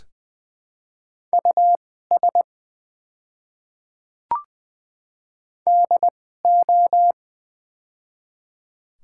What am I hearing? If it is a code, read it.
Morse code, heard as a keyed tone, about 20 wpm, spaced as for 14 wpm, 700 Hz. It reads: US DO
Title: 2x Character Spacing - Top 300 Words - 20wpm